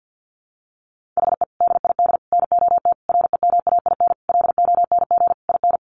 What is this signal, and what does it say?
Morse code, heard as a keyed tone, about 50 wpm, 700 Hz. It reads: HE DID NOT REMAIN LONG IN
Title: Sentences from Top 1500 Words 50wpm